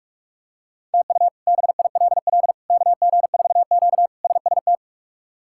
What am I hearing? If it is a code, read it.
Morse code, heard as a keyed tone, about 42 wpm, 700 Hz. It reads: TU BILL KG4Q SST
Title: K1USN SST QSO Practice - 45wpm